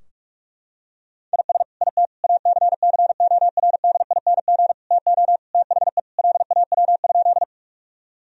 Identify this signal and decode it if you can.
Morse code, heard as a keyed tone, about 46 wpm, 700 Hz. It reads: IS IT ACCORDING TO THE LAW?
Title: Sentences from Top 1000 Words - Review Part 4 - 45wpm